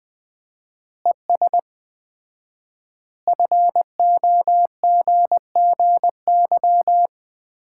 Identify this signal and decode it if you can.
Morse code, heard as a keyed tone, about 20 wpm, 700 Hz. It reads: ES FOGGY
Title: Common QSO Phrases - 4x Word-Spacing - 20wpm